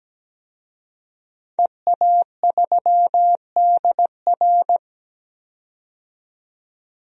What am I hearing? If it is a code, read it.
Morse code, heard as a keyed tone, about 17 wpm, 700 Hz. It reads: EA3DR
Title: DXCC-100 Call Signs - 5 Characters - 17wpm